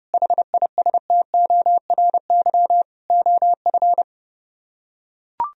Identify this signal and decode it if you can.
Morse code, heard as a keyed tone, about 30 wpm, 700 Hz. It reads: HISTORY OF